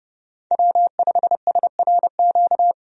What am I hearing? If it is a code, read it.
Morse code, heard as a keyed tone, about 31 wpm, 700 Hz. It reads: W5SRQ
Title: USA Call Signs - 5 Characters - 30wpm